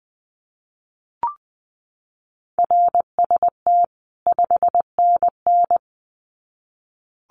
Morse code, heard as a keyed tone, about 20 wpm, 700 Hz. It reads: RST 5NN
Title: Common QSO Phrases 20wpm